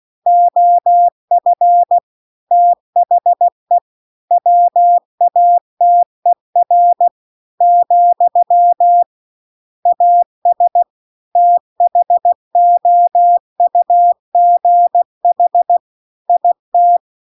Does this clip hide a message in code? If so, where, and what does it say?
Yes: Morse code from 0.3 to 17.0 s, heard as a keyed tone, about 16 wpm, 700 Hz. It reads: OF THE WATER , AS THOUGH IT